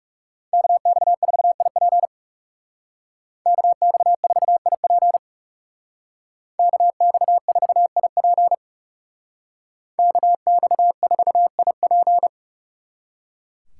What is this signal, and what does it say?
Morse code, heard as a keyed tone, about 39 wpm, 700 Hz. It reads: KX4IP KX4IP KX4IP KX4IP